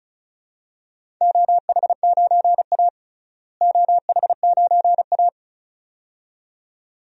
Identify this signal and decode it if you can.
Morse code, heard as a keyed tone, about 36 wpm, 700 Hz. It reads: OH9A OH9A
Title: DXCC-100 Call Signs - 4 Characters - Repeated 2x - 35wpm